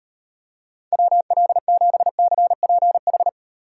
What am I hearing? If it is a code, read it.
Morse code, heard as a keyed tone, about 38 wpm, 700 Hz. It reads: WL7CPH